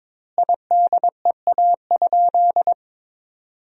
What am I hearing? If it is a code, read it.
Morse code, heard as a keyed tone, about 22 wpm, 700 Hz. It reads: IDEA?
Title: 2x Word-Spacing - Sentences from Top 400 Words - 22wpm